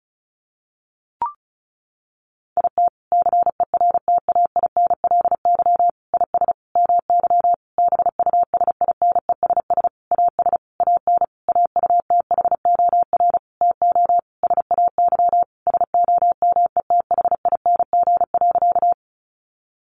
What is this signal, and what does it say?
Morse code, heard as a keyed tone, about 34 wpm, 700 Hz. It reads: IT CERTAINLY IS MY BUSINESS AS AN AUTHOR TO SAY SOMETHING.
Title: Sentences from Top 1200 Words 35wpm